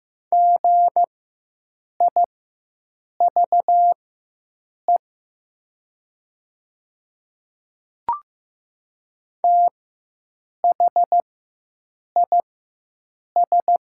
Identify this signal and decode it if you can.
Morse code, heard as a keyed tone, about 15 wpm, spaced as for 7 wpm, 700 Hz. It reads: GIVE THIS
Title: 4x Character Spacing - Top 300 Words - 15wpm